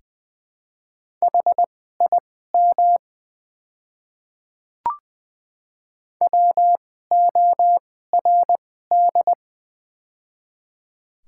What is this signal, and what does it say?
Morse code, heard as a keyed tone, about 20 wpm, spaced as for 14 wpm, 700 Hz. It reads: HIM WORD
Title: 2x Character Spacing - Top 100 Words - 20wpm